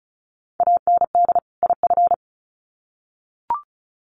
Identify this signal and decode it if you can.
Morse code, heard as a keyed tone, about 34 wpm, 700 Hz. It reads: AND IF
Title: Sets of 2 Words 35wpm